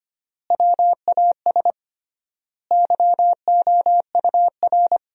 Morse code, heard as a keyed tone, about 25 wpm, 700 Hz. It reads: WAS YOUR
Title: 3x Word-Spacing - Sentences from Top 200 Words - 25wpm